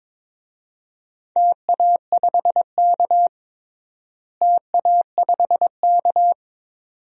Morse code, heard as a keyed tone, about 22 wpm, 700 Hz. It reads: TA5K TA5K
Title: DXCC-100 Call Signs - 4 Characters - Repeated 2x - 22wpm